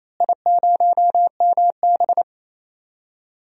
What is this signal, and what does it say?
Morse code, heard as a keyed tone, about 27 wpm, 700 Hz. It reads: I0MB
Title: DXCC-100 Call Signs - 4 Characters - Repeated 3x - 28wpm